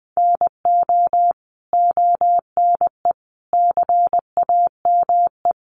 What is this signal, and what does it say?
Morse code, heard as a keyed tone, about 20 wpm, 700 Hz. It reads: NO ONE CAME
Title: Sentences from Top 1300 Words 20wpm